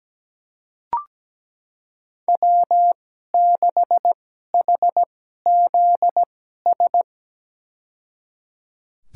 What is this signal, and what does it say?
Morse code, heard as a keyed tone, about 17 wpm, spaced as for 12 wpm, 700 Hz. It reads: W6HZS
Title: USA Call Signs - 5 Characters - 2x Character Spacing - 17wpm